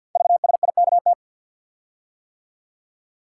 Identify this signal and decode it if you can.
Morse code, heard as a keyed tone, about 51 wpm, 700 Hz. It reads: VSIKT
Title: Code Groups - 5 Letters - 50wpm